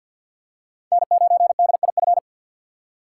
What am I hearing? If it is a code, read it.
Morse code, heard as a keyed tone, about 47 wpm, 700 Hz. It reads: N9DIF
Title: USA Call Signs - 6x Character Spacing - 50wpm